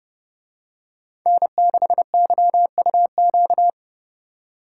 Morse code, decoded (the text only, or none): N6YUQ